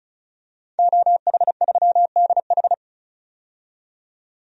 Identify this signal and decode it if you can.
Morse code, heard as a keyed tone, about 36 wpm, 700 Hz. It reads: OH3DH
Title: DXCC-100 Call Signs - 5 Characters - 35wpm